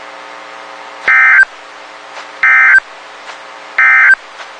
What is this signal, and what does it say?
Emergency Alert System (SAME) tones, heard as harsh buzzing bursts. It is heard from 1.0 to 4.2 s.
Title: WXK91 NOAA Weather Radio